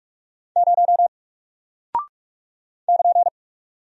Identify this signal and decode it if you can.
Morse code, heard as a keyed tone, about 44 wpm, 700 Hz. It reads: 0 <KN>
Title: Single Character (Letter:Number:Punctuation:Prosign) Alt 45wpm